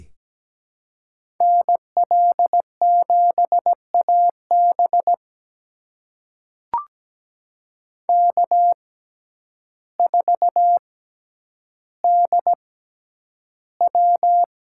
Morse code, heard as a keyed tone, about 17 wpm, spaced as for 9 wpm, 700 Hz. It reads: NL7AB K4DW
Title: USA Call Signs - 6x Character Spacing - 17wpm